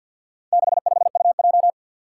Morse code, heard as a keyed tone, about 51 wpm, 700 Hz. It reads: 65UJ
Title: Four Challenging Letter-Number Combinations 50wpm